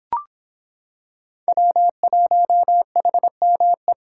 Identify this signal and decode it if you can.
Morse code, heard as a keyed tone, about 26 wpm, 700 Hz. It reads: W1HME